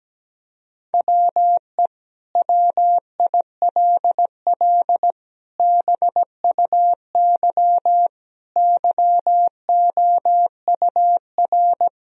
Morse code, heard as a keyed tone, about 17 wpm, 700 Hz. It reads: WE WILL BUY YOUR
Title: Sentences from Top 1000 Words - Review Part 4 - 17wpm